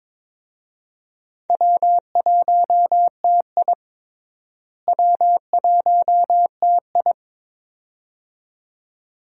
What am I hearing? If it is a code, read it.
Morse code, heard as a keyed tone, about 22 wpm, 700 Hz. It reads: W1TI W1TI
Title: USA Call Signs - 4 Characters - Repeated 2x - 22wpm